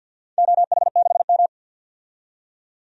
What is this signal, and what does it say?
Morse code, heard as a keyed tone, about 50 wpm, 700 Hz. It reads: OH6M
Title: DXCC-100 Call Signs - 4 Characters - 50wpm